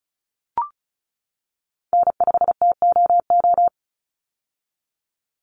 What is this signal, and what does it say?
Morse code, heard as a keyed tone, about 34 wpm, 700 Hz. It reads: N5TOO
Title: Call Signs - 35wpm